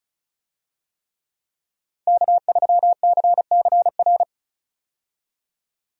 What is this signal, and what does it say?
Morse code, heard as a keyed tone, about 34 wpm, 700 Hz. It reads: K3CCR
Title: USA Call Signs - 5 Characters - Repeated 3x - 35wpm